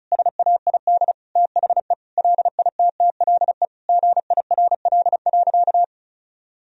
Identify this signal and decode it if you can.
Morse code, heard as a keyed tone, about 36 wpm, 700 Hz. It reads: SAID THE LITTLE GIRL.
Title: Sentences from Top 1000 Words - Review Part 1 - 35wpm